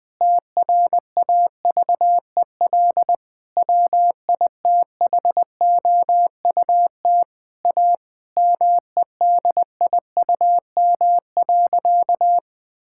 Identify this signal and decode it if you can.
Morse code, heard as a keyed tone, about 20 wpm, 700 Hz. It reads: TRAVEL WITHOUT A MEDIUM.
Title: Sentences from Top 1000 Words - Review Part 6 - 20wpm